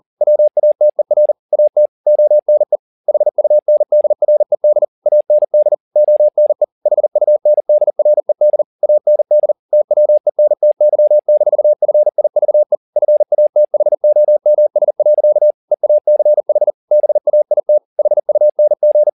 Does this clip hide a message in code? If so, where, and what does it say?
Yes: Morse code from 0.2 to 19.1 s, heard as a keyed tone, about 40 wpm, 600 Hz. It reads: WATER AT ONE HUNDRED AND ONE HUNDRED AND TWENTY-FIVE FATHOMS. EACH BAIT HUNG